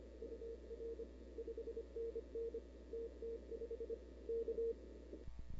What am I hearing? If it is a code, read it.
Morse code, heard as a keyed tone, about 26 wpm, 470 Hz. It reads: UR 5NN TT5 K
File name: CW / 2005-k5tr arrl cw dx -00:00